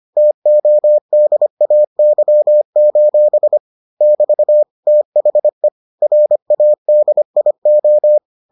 Morse code, heard as a keyed tone, about 25 wpm, 600 Hz. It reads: TODAY: <BT> THE RADIO